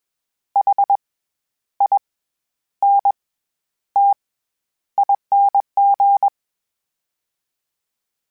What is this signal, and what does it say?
Morse code, heard as a keyed tone, about 21 wpm, 800 Hz. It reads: H I N T ING
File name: Special INGs-mp3s / Special INGs-mp3s / SlowToFaster ing @20